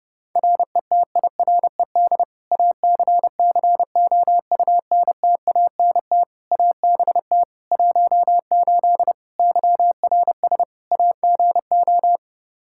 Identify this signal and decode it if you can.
Morse code, heard as a keyed tone, about 30 wpm, 700 Hz. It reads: RETIRED ACCOUNTANT ABT 18 YRS AGO
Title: K1USN SST QSO Practice 30wpm